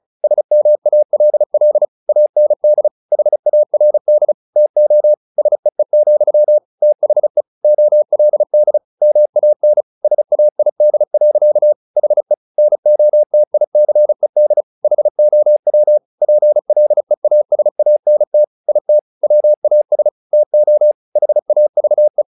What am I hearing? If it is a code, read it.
Morse code, heard as a keyed tone, about 34 wpm, 600 Hz. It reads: SMALL AND HARD TO SEE, THE OLD MAN SAID. HE NOTICED HOW PLEASANT IT WAS TO HAVE
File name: the old man and the sea 2 35wpm